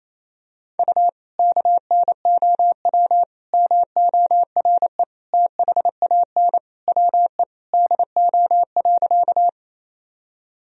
Morse code, heard as a keyed tone, about 28 wpm, 700 Hz. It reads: U KNOW MORE THAN WE DO.
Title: Sentences from Top 100 Words and Abbreviations 28wpm